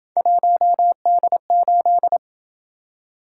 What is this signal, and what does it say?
Morse code, heard as a keyed tone, about 27 wpm, 700 Hz. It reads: 1D8